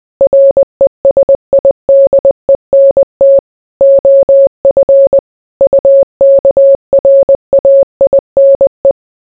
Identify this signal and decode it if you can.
Morse code, heard as a keyed tone, about 20 wpm, 550 Hz. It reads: RESIDENT OF UKRAINE